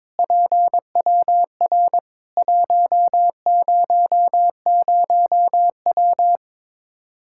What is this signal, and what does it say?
Morse code, heard as a keyed tone, about 22 wpm, 700 Hz. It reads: PWR 100W
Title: Practice QSO - Exchange 2 - Info - 22wpm